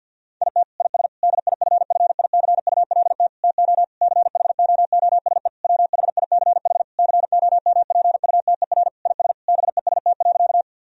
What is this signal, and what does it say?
Morse code, heard as a keyed tone, about 50 wpm, 700 Hz. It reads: IT IS DIFFICULT TO CHOOSE WHICH COMPUTER IS BEST.
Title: Sentences from Top 1000 Words - Review Part 3 - 50wpm